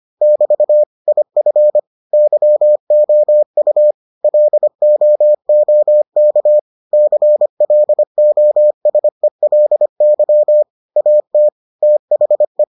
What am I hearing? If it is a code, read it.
Morse code, heard as a keyed tone, about 25 wpm, 600 Hz. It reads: <BT> IF YOU LOOK CLOSELY AT THE